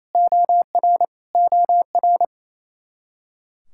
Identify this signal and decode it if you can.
Morse code, heard as a keyed tone, about 28 wpm, 700 Hz. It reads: OR OR
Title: Practice QSO - Exchange 1 - Beginning - 28wpm